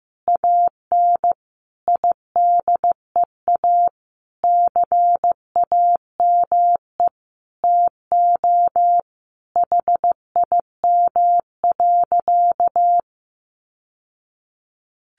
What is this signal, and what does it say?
Morse code, heard as a keyed tone, about 15 wpm, 700 Hz. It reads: AN IDEA CAME TO HIM.
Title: Sentences from Top 1000 Words 15wpm